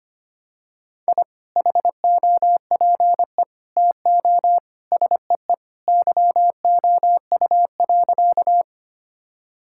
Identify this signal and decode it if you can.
Morse code, heard as a keyed tone, about 25 wpm, 700 Hz. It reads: I HOPE TO SEE YOU.
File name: practice / 25wpm / Sentences with 2-Word Phrases - 25wpm